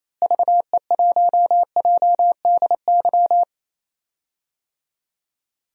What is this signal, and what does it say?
Morse code, heard as a keyed tone, about 27 wpm, 700 Hz. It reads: VE1JDY
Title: DXCC-100 Call Signs - 6 Characters - Repeated 2x - 28wpm